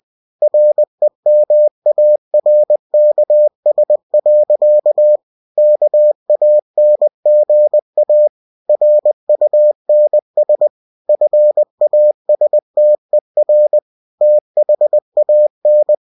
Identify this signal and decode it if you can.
Morse code, heard as a keyed tone, about 20 wpm, 600 Hz. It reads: REMARKS. KANGA RUNS FASTER THAN
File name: cw-books / winnie-the-pooh / winnie-20wpm-page-0063